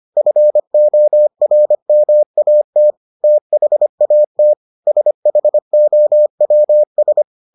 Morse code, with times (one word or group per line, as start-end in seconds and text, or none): 0.2-2.9 FORMAT
3.2-4.5 THAT
4.9-7.2 SHOWS